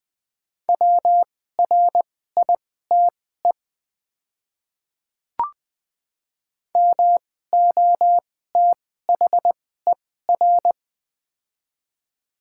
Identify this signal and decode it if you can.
Morse code, heard as a keyed tone, about 20 wpm, spaced as for 14 wpm, 700 Hz. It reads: WRITE MOTHER
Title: 2x Character Spacing - Top 200 Words - 20wpm